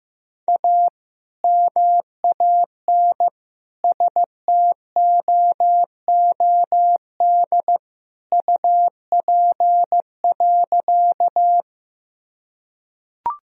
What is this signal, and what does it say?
Morse code, heard as a keyed tone, about 16 wpm, 700 Hz. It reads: A MAN STOOD UP.
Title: Sentences from Top 1000 Words - Review Part 2 - 15wpm